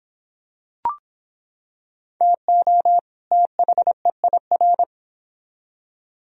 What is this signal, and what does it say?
Morse code, heard as a keyed tone, about 25 wpm, 700 Hz. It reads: TO THEIR